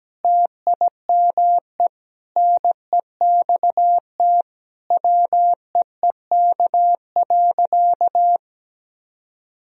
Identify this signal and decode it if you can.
Morse code, heard as a keyed tone, about 17 wpm, 700 Hz. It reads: TIME NEXT WEEK.
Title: Sentences from Top 1000 Words - Review Part 2 - 17wpm